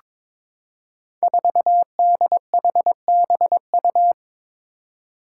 Morse code, heard as a keyed tone, about 22 wpm, 700 Hz. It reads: 4DHBU
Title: Five Challenging Letter-Number Combinations 22wpm